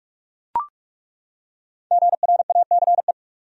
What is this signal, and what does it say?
Morse code, heard as a keyed tone, about 42 wpm, 700 Hz. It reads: GRACE
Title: USA Common Names - 45wpm